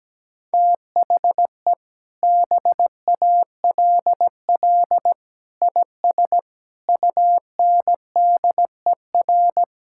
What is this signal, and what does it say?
Morse code, heard as a keyed tone, about 17 wpm, 700 Hz. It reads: THE BALL IS UNDER
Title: Sentences from Top 1000 Words - Review Part 6 - 17wpm